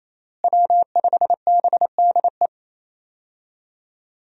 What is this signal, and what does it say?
Morse code, heard as a keyed tone, about 28 wpm, 700 Hz. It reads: W5BDE
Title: USA Call Signs - 5 Characters - Repeated 2x - 28wpm